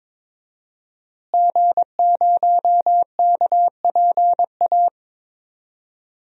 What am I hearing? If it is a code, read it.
Morse code, heard as a keyed tone, about 22 wpm, 700 Hz. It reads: G0KPA
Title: DXCC-100 Call Signs - 5 Characters - Repeated 2x - 22wpm